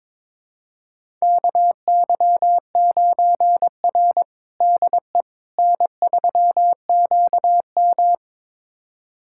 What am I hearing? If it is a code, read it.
Morse code, heard as a keyed tone, about 22 wpm, 700 Hz. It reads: KY9R DE N3QM
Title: USA Callsign DE Callsign - 4 Characters - 22wpm